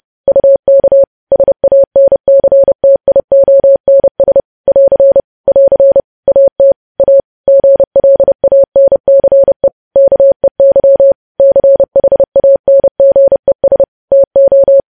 Morse code, heard as a keyed tone, about 30 wpm, 550 Hz. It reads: UK SANCTIONS <AR> <AR> AT A GLANCE KEY CHANGES TO
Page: news headlines in morse code at 30 wpm